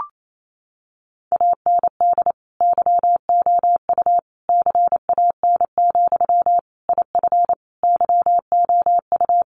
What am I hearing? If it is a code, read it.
Morse code, heard as a keyed tone, about 28 wpm, 700 Hz. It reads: AND YOU CAN, IF YOU
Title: Sentences from Top 1500 Words 28wpm